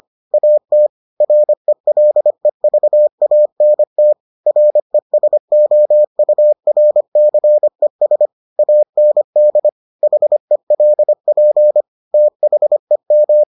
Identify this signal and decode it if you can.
Morse code, heard as a keyed tone, about 25 wpm, 600 Hz. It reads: AT RELEVANT RESOURCES AND HELP THEM